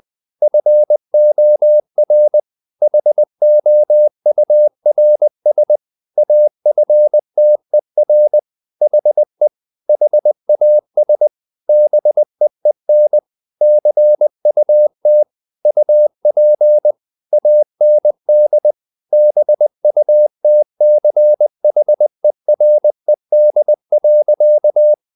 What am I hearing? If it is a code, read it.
Morse code, heard as a keyed tone, about 20 wpm, 600 Hz. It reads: FOR HOURS AFTER HE HAS BEEN CUT UP AND BUTCHERED.
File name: the old man and the sea / the old man and the sea part 1 20wpm